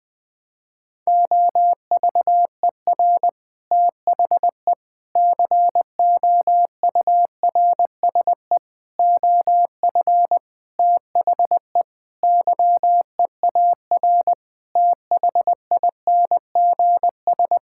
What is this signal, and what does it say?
Morse code, heard as a keyed tone, about 20 wpm, 700 Hz. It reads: OVER THE COURSE OF THE YEAR THINGS
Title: Sentences with 5-Word Phrases - Spoken Beginning - 20wpm